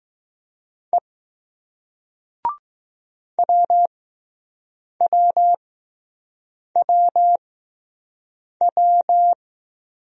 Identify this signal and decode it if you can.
Morse code, heard as a keyed tone, about 20 wpm, spaced as for 6 wpm, 700 Hz. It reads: E WWWW